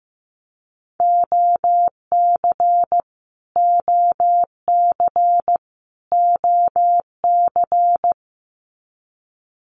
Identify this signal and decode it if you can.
Morse code, heard as a keyed tone, about 15 wpm, 700 Hz. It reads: OC OC OC
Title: Abbreviations Repeated 3x - 15wpm